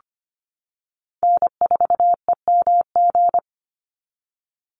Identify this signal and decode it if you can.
Morse code, heard as a keyed tone, about 25 wpm, 700 Hz. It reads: N4EMG
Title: NA SOTA Chaser Callsigns 25wpm